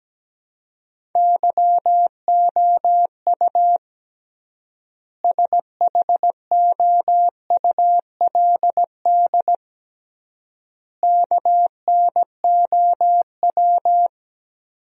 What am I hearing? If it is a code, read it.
Morse code, heard as a keyed tone, about 17 wpm, 700 Hz. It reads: YOU SHOULD KNOW
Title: 3x Word-Spacing - Sentences from Top 400 Words - 17wpm